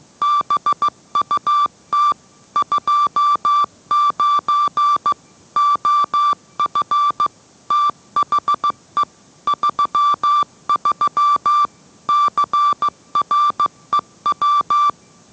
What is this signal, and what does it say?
Morse code, heard as a keyed tone, about 15 wpm, 1170 Hz. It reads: BUT 29 OF THE 33 CREW